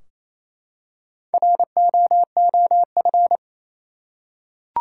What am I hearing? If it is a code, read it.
Morse code, heard as a keyed tone, about 28 wpm, 700 Hz. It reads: ROOF
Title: Four Letter Words - 28wpm